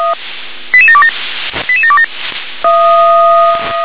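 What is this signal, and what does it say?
Five-tone selective call (ZVEI), heard as quick runs of five tones, tones of about 70 ms. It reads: 88727 88727
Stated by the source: ZVEI five-tone selective call 88926 received by SDR.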